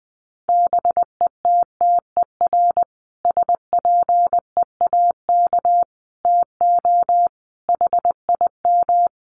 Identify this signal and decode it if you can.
Morse code, heard as a keyed tone, about 20 wpm, 700 Hz. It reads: BETTER SPEAK TO HIM
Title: Sentences from Top 1100 Words 20wpm